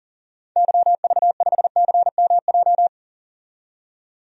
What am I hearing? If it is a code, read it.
Morse code, heard as a keyed tone, about 40 wpm, 700 Hz. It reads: YV5CMJ
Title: DXCC-100 Call Signs - 6 Characters - 40wpm